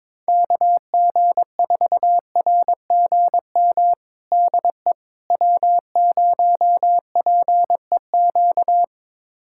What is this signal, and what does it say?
Morse code, heard as a keyed tone, about 22 wpm, 700 Hz. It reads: KG4RGM DE W0PEQ
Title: USA Callsign DE Callsign 22wpm